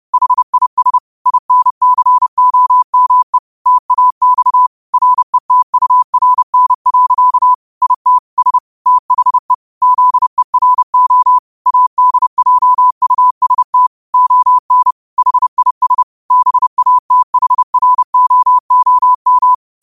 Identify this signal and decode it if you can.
Morse code, heard as a keyed tone, about 30 wpm, 1000 Hz. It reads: HIS INCOME TAX RETURN. IT S THE ZERO ADJUST ON HIS BATHROOM
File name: qotdfiles / 30wpm